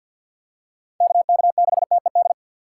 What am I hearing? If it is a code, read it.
Morse code, heard as a keyed tone, about 51 wpm, 700 Hz. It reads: KK6TED
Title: USA Call Signs - 6 Characters - 3x Character Spacing - 50wpm